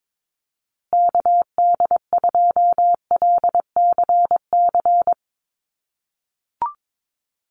Morse code, heard as a keyed tone, about 22 wpm, 700 Hz. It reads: KD2LCC
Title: USA Call Signs Repeated 2x 22wpm